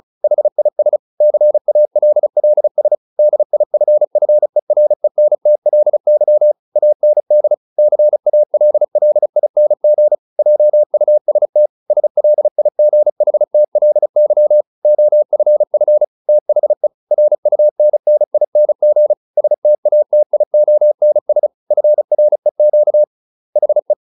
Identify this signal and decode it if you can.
Morse code, heard as a keyed tone, about 34 wpm, 600 Hz. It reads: HIS CALLS DIFFERENTLY AND CALLING JUST SLIGHTLY OFF THE RUNNING STATIONS FREQ HE